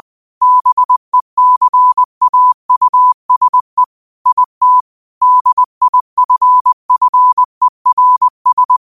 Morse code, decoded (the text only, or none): BECAUSE IT DIFFERS